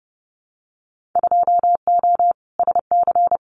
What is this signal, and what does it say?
Morse code, heard as a keyed tone, about 30 wpm, 700 Hz. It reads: 2O SC
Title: Winter Field Day Exchanges 30wpm